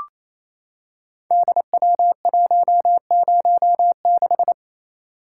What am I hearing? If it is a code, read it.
Morse code, heard as a keyed tone, about 27 wpm, 700 Hz. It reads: DW106
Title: Code Groups - 5 Letter:Numbers - 28wpm